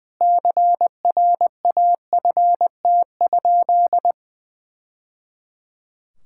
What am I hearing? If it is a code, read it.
Morse code, heard as a keyed tone, about 21 wpm, 700 Hz. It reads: CRAFT?
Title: Sentences from Top 1000 Words - Review Part 6 - 20wpm